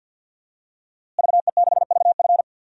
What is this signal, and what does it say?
Morse code, heard as a keyed tone, about 47 wpm, 700 Hz. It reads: VE6VF
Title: DXCC-100 Call Signs - 5 Characters - Repeated 2x - 50wpm